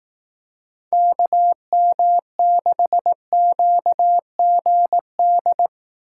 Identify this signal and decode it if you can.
Morse code, heard as a keyed tone, about 18 wpm, 700 Hz. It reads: KM6QGD
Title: Speed Racing - USA 2x3 Call Signs 15wpm